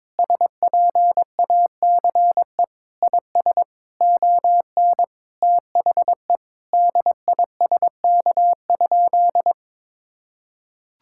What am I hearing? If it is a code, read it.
Morse code, heard as a keyed tone, about 22 wpm, 700 Hz. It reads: SPACE IS ON THE DISK?
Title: Sentences from Top 1000 Words - Review Part 6 - 22wpm